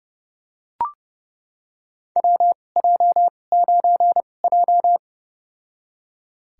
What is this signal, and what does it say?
Morse code, heard as a keyed tone, about 32 wpm, spaced as for 22 wpm, 700 Hz. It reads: WJ9J
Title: USA Call Signs - 4 Characters - 2x Character Spacing - 30wpm